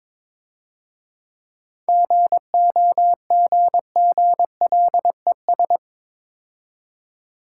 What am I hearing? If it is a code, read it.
Morse code, heard as a keyed tone, about 26 wpm, 690 Hz. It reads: GOGGLES
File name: Easy Words - Speed-Racing with Hints - 22wpm